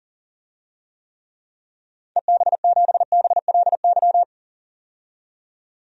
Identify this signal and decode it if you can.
Morse code, heard as a keyed tone, about 40 wpm, 700 Hz. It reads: EB7BLY
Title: DXCC-100 Call Signs - 6 Characters - 40wpm